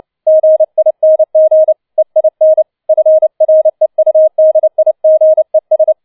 Morse code, heard as a keyed tone, about 29 wpm, 630 Hz. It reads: GING EIN FREUDIGES
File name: 007 Letzter Streich 30wpm